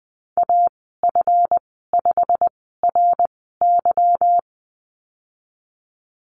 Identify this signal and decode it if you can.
Morse code, heard as a keyed tone, about 20 wpm, spaced as for 15 wpm, 700 Hz. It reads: AF5RY
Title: 186 Lesson 30g - Letter b - Call Signs 20wpm